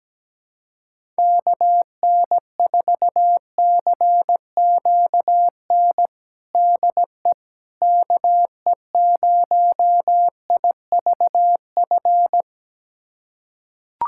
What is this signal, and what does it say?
Morse code, heard as a keyed tone, about 17 wpm, 700 Hz. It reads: KN4CQN DE KE0IVF
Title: USA Callsign DE Callsign - 6 Characters - 17wpm